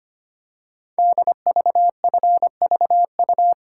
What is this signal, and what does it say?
Morse code, heard as a keyed tone, about 25 wpm, 700 Hz. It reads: DVFVU
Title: Code Groups - 5 Letters - 25wpm